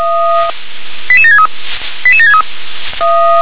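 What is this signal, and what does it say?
Five-tone selective call (ZVEI), heard as quick runs of five tones, tones of about 70 ms. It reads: 88763 88763
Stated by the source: Five-tone ZVEI selective call received by SDR.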